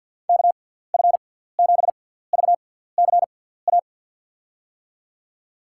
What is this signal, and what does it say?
Morse code, heard as a keyed tone, about 47 wpm, spaced as for 17 wpm, 700 Hz. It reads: KF7VCA
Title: USA Call Signs - 6 Characters - 6x Character Spacing - 50wpm